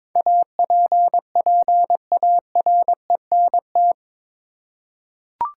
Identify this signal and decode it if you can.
Morse code, heard as a keyed tone, about 23 wpm, 700 Hz. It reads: APPARENT
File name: practice / 22wpm / Eight Letter Words - 22wpm